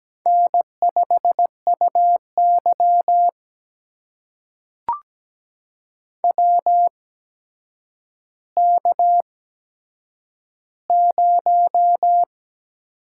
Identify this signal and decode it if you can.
Morse code, heard as a keyed tone, about 17 wpm, spaced as for 8 wpm, 700 Hz. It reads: N5UY WK0
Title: USA Call Signs - 4 Characters - 8x Character Spacing - 17wpm